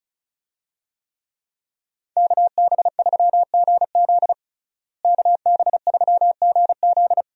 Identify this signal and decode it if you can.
Morse code, heard as a keyed tone, about 34 wpm, 700 Hz. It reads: KB3GZ KB3GZ
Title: USA Call Signs - 5 Characters - Repeated 2x - 35wpm